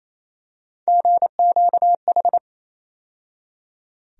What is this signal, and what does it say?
Morse code, heard as a keyed tone, about 28 wpm, 700 Hz. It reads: GQH
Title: USA Call Sign Postfixes - 3 Letters 28wpm